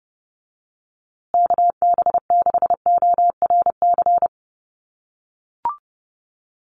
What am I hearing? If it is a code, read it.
Morse code, heard as a keyed tone, about 30 wpm, 700 Hz. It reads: KB6ORC
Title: USA Call Signs Repeated 2x 30wpm